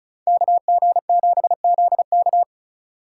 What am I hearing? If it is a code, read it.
Morse code, heard as a keyed tone, about 43 wpm, 700 Hz. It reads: KG7ZK
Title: Speed Racing - USA 2x2 Call Signs 35wpm